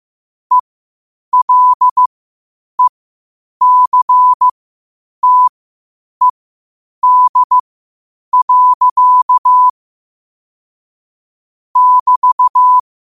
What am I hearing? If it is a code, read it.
Morse code, heard as a keyed tone, about 15 wpm, spaced as for 8 wpm, 1000 Hz. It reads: ELECTED. <BT>
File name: qotdfiles / 8wpm